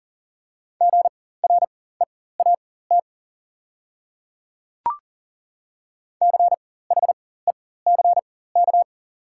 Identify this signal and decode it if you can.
Morse code, heard as a keyed tone, about 40 wpm, spaced as for 19 wpm, 690 Hz. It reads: GREAT CHECK
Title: 4x Character Spacing - Top 500 Words - 40wpm